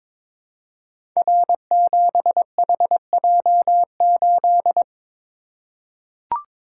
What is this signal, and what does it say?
Morse code, heard as a keyed tone, about 22 wpm, 700 Hz. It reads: R7HJ8
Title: Code Groups - 5 Letter:Numbers - 22wpm